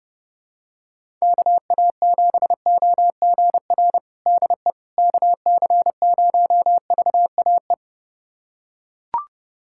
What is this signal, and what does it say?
Morse code, heard as a keyed tone, about 30 wpm, 700 Hz. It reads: KA7OGR DE KC0VAE
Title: USA Callsign DE Callsign - 6 Characters - 30wpm